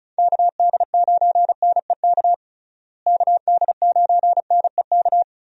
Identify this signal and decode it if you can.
Morse code, heard as a keyed tone, about 34 wpm, 700 Hz. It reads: KD9NEK KD9NEK
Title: USA Call Signs - 6 Characters - Repeated 3x - 35wpm